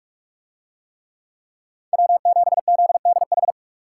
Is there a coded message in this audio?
Yes: Morse code from 1.9 to 3.5 s, heard as a keyed tone, about 40 wpm, 700 Hz. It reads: W7ZDH